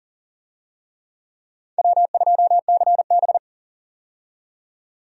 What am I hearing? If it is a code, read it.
Morse code, heard as a keyed tone, about 40 wpm, 700 Hz. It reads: W2CB
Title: USA Call Signs - 4 Characters - 40wpm